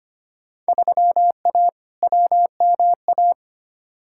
Morse code, heard as a keyed tone, about 25 wpm, 700 Hz. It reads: 3A WMA
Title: Field Day Exchanges - 25wpm